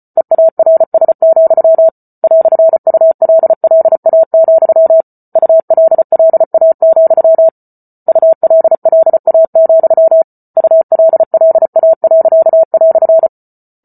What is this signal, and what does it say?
Morse code, heard as a keyed tone, about 34 wpm, 650 Hz. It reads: EARS, "ULLA, ULLA, ULLA, ULLA."